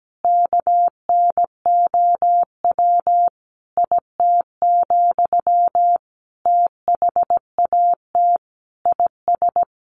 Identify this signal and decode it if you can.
Morse code, heard as a keyed tone, about 17 wpm, 700 Hz. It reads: KNOW IT, THAT IS
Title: Sentences from Top 400 Words 17wpm